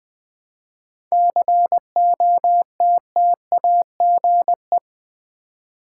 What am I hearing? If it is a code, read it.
Morse code, heard as a keyed tone, about 21 wpm, 700 Hz. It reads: COTTAGE
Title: 7 Letter Words - First 3 Letters Given - 20wpm